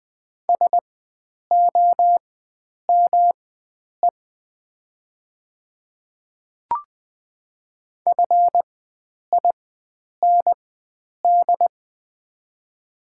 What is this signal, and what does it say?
Morse code, heard as a keyed tone, about 20 wpm, spaced as for 9 wpm, 700 Hz. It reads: SOME FIND
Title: 4x Character Spacing - Top 100 Words - 20wpm